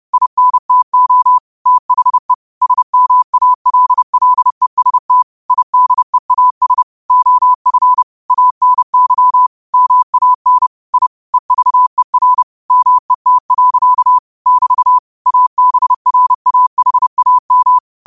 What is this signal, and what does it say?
Morse code, heard as a keyed tone, about 30 wpm, 1000 Hz. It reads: INTO THE SMALLEST IDEAS OF ANY MAN I EVER MET. <BT> ABRAHAM
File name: qotdfiles / 30wpm